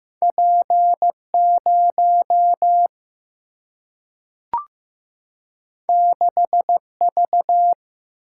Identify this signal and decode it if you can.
Morse code, heard as a keyed tone, about 15 wpm, 700 Hz. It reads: P 0 6 V
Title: Single-Letter Single-Number 15wpm